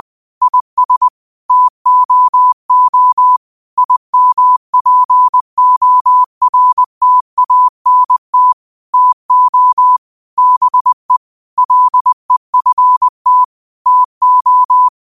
Morse code, heard as a keyed tone, about 20 wpm, 1000 Hz. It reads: IS TOO IMPORTANT TO BE LEFT TO
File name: qotdfiles / 20wpm